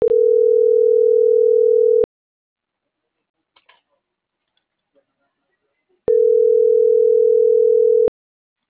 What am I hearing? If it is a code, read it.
Ringback tone (North America), 440 + 480 Hz.